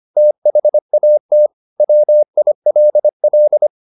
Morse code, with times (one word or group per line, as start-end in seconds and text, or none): 0.2-1.5 THAT
1.8-3.7 WILL